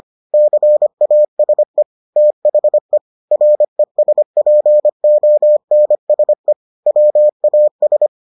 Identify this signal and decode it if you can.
Morse code, heard as a keyed tone, about 25 wpm, 600 Hz. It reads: CASE THE RESPONSE WAS